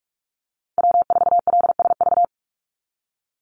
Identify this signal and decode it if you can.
Morse code, heard as a keyed tone, about 44 wpm, 700 Hz. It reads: W4LSV
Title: Call Signs - 45wpm